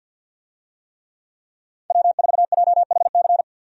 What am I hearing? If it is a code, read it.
Morse code, heard as a keyed tone, about 52 wpm, 700 Hz. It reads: W4JHC
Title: USA Call Signs - 5 Characters - 50wpm